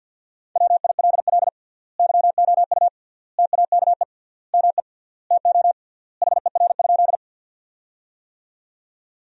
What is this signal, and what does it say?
Morse code, heard as a keyed tone, about 52 wpm, 700 Hz. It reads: WILL YOU TAKE ME TO HER?
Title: 3x Word-Spacing - Sentences from Top 200 Words - 50wpm